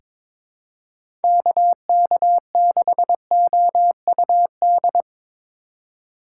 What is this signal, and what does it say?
Morse code, heard as a keyed tone, about 23 wpm, 700 Hz. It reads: KK6OUD
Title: USA Call Signs - 6 Characters - 3x Character Spacing - 22wpm